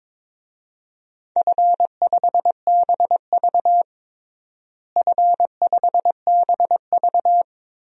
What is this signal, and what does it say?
Morse code, heard as a keyed tone, about 22 wpm, 700 Hz. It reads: F5BV F5BV
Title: DXCC-100 Call Signs - 4 Characters - Repeated 2x - 22wpm